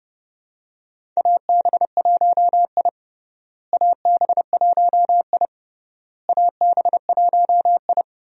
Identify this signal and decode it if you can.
Morse code, heard as a keyed tone, about 32 wpm, 700 Hz. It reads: AB1I AB1I AB1I
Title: USA Call Signs - 4 Characters - Repeated 3x - 30wpm